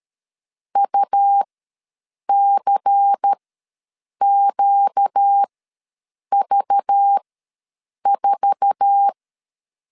Morse code, heard as a keyed tone, about 13 wpm, spaced as for 7 wpm, 800 Hz. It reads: UCQV4
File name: LEZIONI / 8 YJ45 / 8 YJ4 ws 12wpm 3249